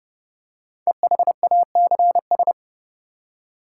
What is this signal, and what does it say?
Morse code, heard as a keyed tone, about 30 wpm, 700 Hz. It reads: EHACS
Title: Code Groups - 5 Letters - 30wpm